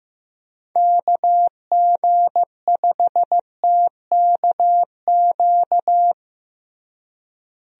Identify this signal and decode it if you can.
Morse code, heard as a keyed tone, about 16 wpm, 700 Hz. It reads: KG5TKQ
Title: USA Call Signs - 6 Characters - Repeated 2x - 15wpm